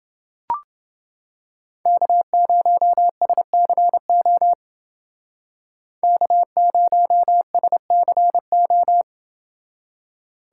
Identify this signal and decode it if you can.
Morse code, heard as a keyed tone, about 28 wpm, 700 Hz. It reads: K0SCO K0SCO